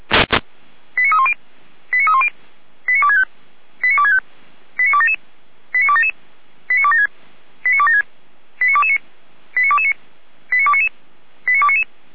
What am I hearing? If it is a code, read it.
Five-tone selective call (ZVEI), heard as quick runs of five tones, tones of about 70 ms. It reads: 89310 89310 89265 89265 89277 89277 89276 89276 89229 89229 89299 89299